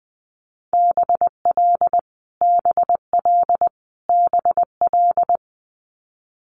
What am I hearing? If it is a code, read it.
Morse code, heard as a keyed tone, about 20 wpm, 700 Hz. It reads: BL BL BL
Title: Multi-letter Phonemes Repeated 3x 20wpm